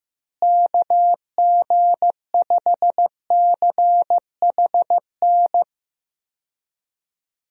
Morse code, heard as a keyed tone, about 15 wpm, 700 Hz. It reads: KG5CHN
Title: USA Call Signs - 6 Characters - Repeated 3x - 15wpm